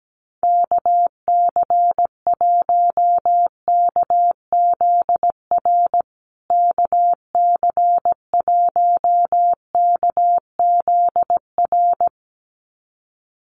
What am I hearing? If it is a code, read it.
Morse code, heard as a keyed tone, about 17 wpm, 700 Hz. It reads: KC1KZR KC1KZR
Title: USA Call Signs Repeated 2x 17wpm